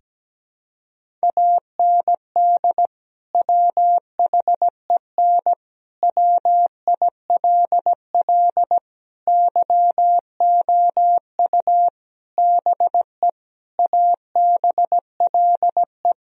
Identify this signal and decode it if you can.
Morse code, heard as a keyed tone, about 17 wpm, 700 Hz. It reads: AND WHEN WILL YOU BE ABLE
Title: Sentences from Top 1000 Words - Review Part 2 - 17wpm